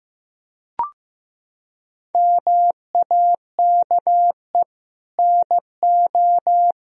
Morse code, heard as a keyed tone, about 15 wpm, 700 Hz. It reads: MAKE NO
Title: Sentences from Top 1000 Words - Review Part 1 - 15wpm